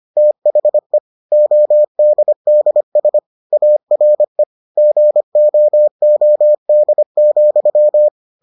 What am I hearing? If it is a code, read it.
Morse code, heard as a keyed tone, about 25 wpm, 600 Hz. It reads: THE ODDS ARE GOOD,